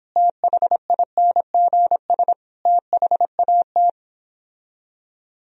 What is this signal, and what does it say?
Morse code, heard as a keyed tone, about 26 wpm, 700 Hz. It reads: THINGS THAT